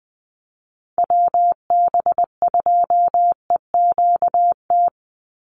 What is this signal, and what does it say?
Morse code, heard as a keyed tone, about 20 wpm, 700 Hz. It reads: WB2EQT
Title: USA Call Signs 20wpm